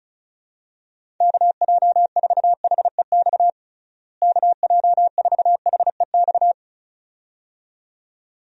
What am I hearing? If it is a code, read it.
Morse code, heard as a keyed tone, about 36 wpm, 700 Hz. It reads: KJ4HEX KJ4HEX
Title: USA Call Signs - 6 Characters - Repeated 2x - 35wpm